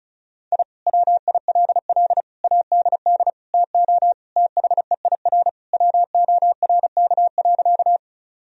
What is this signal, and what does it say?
Morse code, heard as a keyed tone, about 36 wpm, 700 Hz. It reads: I WILL ADD TO THEIR WORK.
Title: Sentences from Top 1000 Words - Review Part 1 - 35wpm